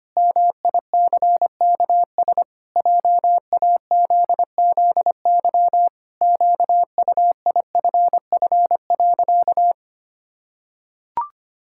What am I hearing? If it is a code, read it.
Morse code, heard as a keyed tone, about 25 wpm, 700 Hz. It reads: MICKS JAZZY QUIFF.
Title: Pangram Challenge v2 - 25wpm